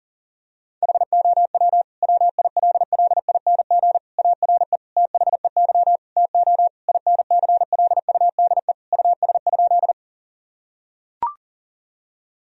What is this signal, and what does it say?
Morse code, heard as a keyed tone, about 40 wpm, 700 Hz. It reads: HOW WILLING ARE THEY TO INCLUDE US?
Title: Sentences from Top 1000 Words - Review Part 3 - 40wpm